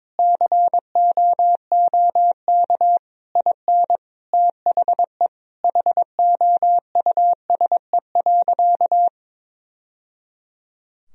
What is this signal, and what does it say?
Morse code, heard as a keyed tone, about 22 wpm, 700 Hz. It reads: COOK IN THE HOUSE.
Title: Sentences from Top 1000 Words - Review Part 4 - 22wpm